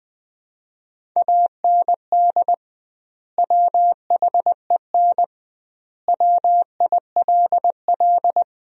Morse code, heard as a keyed tone, about 20 wpm, 700 Hz. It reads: AND WHEN WILL
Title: 2x Word-Spacing - Sentences from Top 100 Words - 20wpm